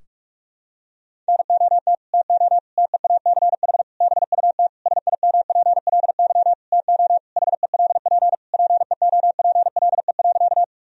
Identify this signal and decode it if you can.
Morse code, heard as a keyed tone, about 46 wpm, 700 Hz. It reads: NOT TO TEACH BUT SIMPLY TO HELP PEOPLE.
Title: Sentences from Top 1000 Words - Review Part 4 - 45wpm